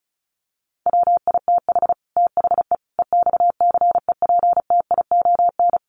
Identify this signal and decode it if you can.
Morse code, heard as a keyed tone, about 34 wpm, 700 Hz. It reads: WITH THE EXCEPTION
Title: Sets of 4 Words 35wpm